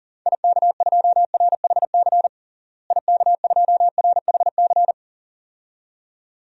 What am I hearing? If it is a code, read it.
Morse code, heard as a keyed tone, about 40 wpm, 700 Hz. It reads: IK2RHC IK2RHC
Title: DXCC-100 Call Signs - 6 Characters - Repeated 2x - 40wpm